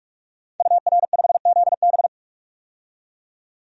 Morse code, heard as a keyed tone, about 41 wpm, 700 Hz. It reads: UR5ZB
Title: DXCC-100 Call Signs - 5 Characters - 45wpm